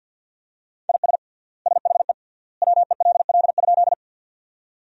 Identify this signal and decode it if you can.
Morse code, heard as a keyed tone, about 50 wpm, 700 Hz. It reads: IS SHE WELL?
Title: 3x Word-Spacing - Sentences from Top 200 Words - 50wpm